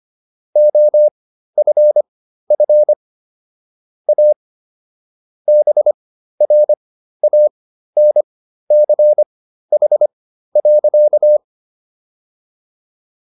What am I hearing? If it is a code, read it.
Morse code, heard as a keyed tone, about 25 wpm, spaced as for 13 wpm, 600 Hz. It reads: OFF A BRANCH.